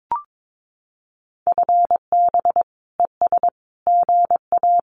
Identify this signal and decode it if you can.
Morse code, heard as a keyed tone, about 22 wpm, 700 Hz. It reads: FB ES GA
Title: Common QSO Phrases Encore 22wpm